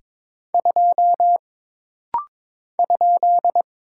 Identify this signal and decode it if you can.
Morse code, heard as a keyed tone, about 22 wpm, 700 Hz. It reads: 2 ?
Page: This rapid-fire timing is geared to push your ICR (Instant Character Recognition) skills.